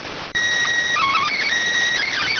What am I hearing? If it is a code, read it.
SSTV (slow-scan television), PD 290.